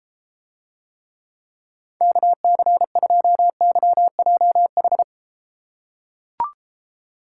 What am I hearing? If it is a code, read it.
Morse code, heard as a keyed tone, about 32 wpm, 700 Hz. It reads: KC2YJH